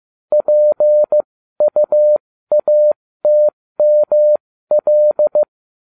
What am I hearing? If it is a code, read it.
Morse code, heard as a keyed tone, about 15 wpm, spaced as for 13 wpm, 600 Hz. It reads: PUATML